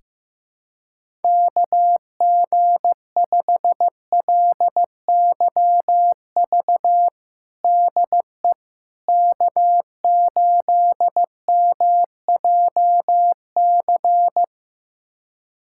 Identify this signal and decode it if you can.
Morse code, heard as a keyed tone, about 15 wpm, 700 Hz. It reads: KG5LYV DE K8MJC
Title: USA Callsign DE Callsign 15wpm